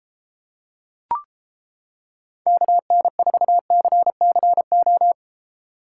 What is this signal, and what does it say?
Morse code, heard as a keyed tone, about 28 wpm, 700 Hz. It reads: KN4CCO